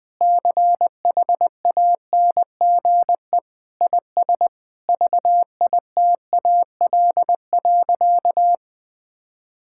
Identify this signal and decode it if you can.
Morse code, heard as a keyed tone, about 20 wpm, 700 Hz. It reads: CHANGE IS VITAL.
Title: Sentences with 8-Word Phrases - 20wpm